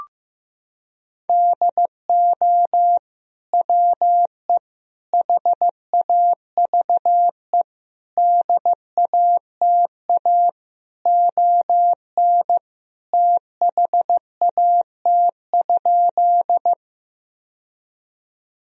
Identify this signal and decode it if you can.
Morse code, heard as a keyed tone, about 15 wpm, 700 Hz. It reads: DO WE HAVE DATA ON THAT?
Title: Sentences from Top 1000 Words - Review Part 3 - 15wpm